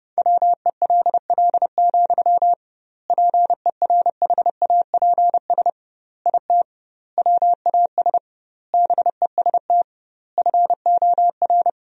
Morse code, heard as a keyed tone, about 30 wpm, 700 Hz. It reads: WELL, PERHAPS IT WAS BEST FOR
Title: 2x Word-Spacing - Sentences from Top 500 Words - 30wpm